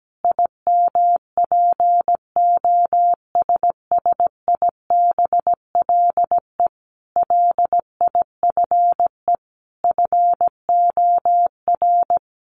Morse code, heard as a keyed tone, about 17 wpm, 700 Hz. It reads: IMPOSSIBLE LIFE FOR
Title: Sentences from Top 1100 Words 17wpm